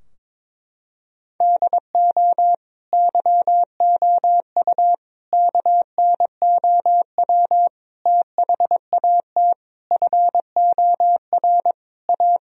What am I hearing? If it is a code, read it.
Morse code, heard as a keyed tone, about 23 wpm, 700 Hz. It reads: DO YOU KNOW THAT FOR A
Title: Sentences from Top 1000 Words - Review Part 3 - 22wpm